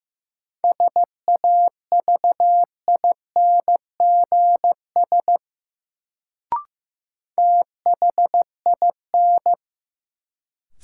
Morse code, heard as a keyed tone, about 16 wpm, 700 Hz. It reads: SAVINGS THIN
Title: Sentences from Top 1200 Words - New Word Drill Alt - 15wpm